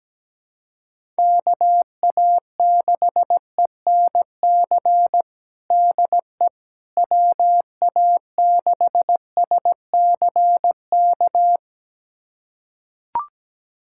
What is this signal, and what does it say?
Morse code, heard as a keyed tone, about 17 wpm, 700 Hz. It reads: KA6ENC DE WA6SCK
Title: USA Callsign DE Callsign - 6 Characters - 17wpm